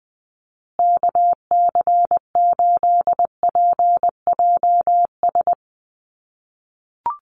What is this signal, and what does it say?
Morse code, heard as a keyed tone, about 20 wpm, 700 Hz. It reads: KC8PJS
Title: USA Call Signs Repeated 3x 20wpm